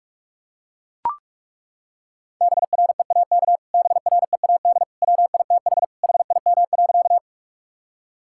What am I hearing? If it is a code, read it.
Morse code, heard as a keyed tone, about 42 wpm, 700 Hz. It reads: BREAK BREAD WITH HIM.
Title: Sentences from Top 1600 Words 45wpm